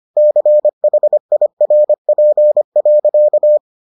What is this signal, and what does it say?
Morse code, heard as a keyed tone, about 25 wpm, 600 Hz. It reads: CHIRP.